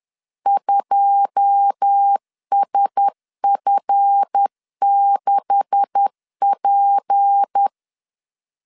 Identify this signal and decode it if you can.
Morse code, heard as a keyed tone, about 10 wpm, 800 Hz. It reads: 2SF6P
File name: LEZIONI / 10 SIGNS / 10D COMA 10wpm 3013